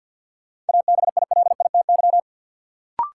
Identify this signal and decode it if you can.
Morse code, heard as a keyed tone, about 47 wpm, 700 Hz. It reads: ABILITY
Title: Sentences from Top 1000 Words - Review Part 3 - New Word Drill - 50wpm